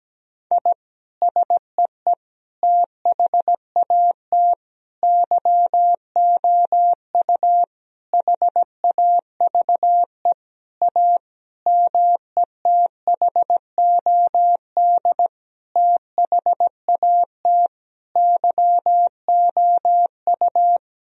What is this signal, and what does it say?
Morse code, heard as a keyed tone, about 17 wpm, 700 Hz. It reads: I SEE THAT YOU HAVE A METHOD THAT YOU
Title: Sentences from Top 1000 Words - Review Part 3 - 17wpm